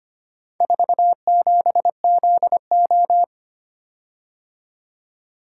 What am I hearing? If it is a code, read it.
Morse code, heard as a keyed tone, about 25 wpm, 700 Hz. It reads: 47ZO
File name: practice / 25wpm / Four Letter-Number Combinations - 25wpm